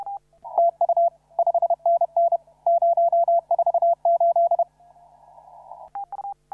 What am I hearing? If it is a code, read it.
Morse code, heard as a keyed tone, about 30 wpm, 690 Hz. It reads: TU 5NN 048
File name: CW / RHOB CW 2017 HA6P 0001